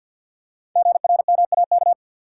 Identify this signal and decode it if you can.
Morse code, heard as a keyed tone, about 48 wpm, 700 Hz. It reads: GRMAK
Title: Code Groups - 5 Letters - 50wpm